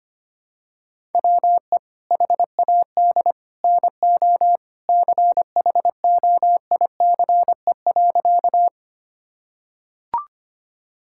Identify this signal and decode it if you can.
Morse code, heard as a keyed tone, about 25 wpm, 700 Hz. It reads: WE HAD NO CHOICE.
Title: Sentences from Top 1000 Words - Review Part 6 - 25wpm